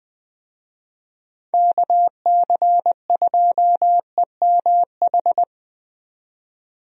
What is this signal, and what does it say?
Morse code, heard as a keyed tone, about 20 wpm, 700 Hz. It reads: KC2EMH
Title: USA Call Signs - 6 Characters - 20wpm